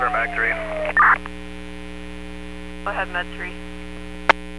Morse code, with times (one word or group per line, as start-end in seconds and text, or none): none